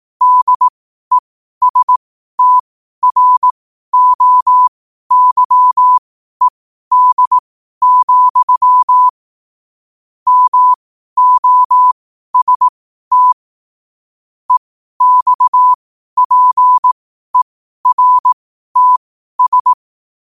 Morse code, heard as a keyed tone, about 18 wpm, spaced as for 12 wpm, 1000 Hz. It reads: DESTROYED, MOST EXPERTS
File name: qotdfiles / 12wpm